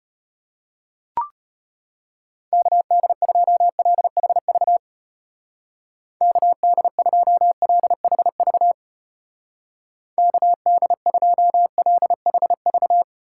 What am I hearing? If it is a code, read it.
Morse code, heard as a keyed tone, about 32 wpm, 700 Hz. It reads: KD2LHV KD2LHV KD2LHV